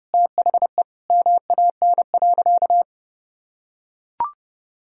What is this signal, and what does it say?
Morse code, heard as a keyed tone, about 30 wpm, 700 Hz. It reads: THE MAN.
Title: Sentences from Top 200 Words and Abbreviations 30wpm